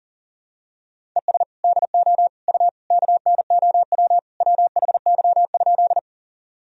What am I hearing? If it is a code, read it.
Morse code, heard as a keyed tone, about 40 wpm, 700 Hz. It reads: ES DO U KNOW WHY?
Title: Sentences from Top 200 Words and Abbreviations 40wpm